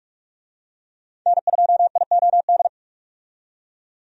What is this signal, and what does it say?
Morse code, heard as a keyed tone, about 46 wpm, 700 Hz. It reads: N2IOD
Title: USA Call Signs - 5 Characters - 45wpm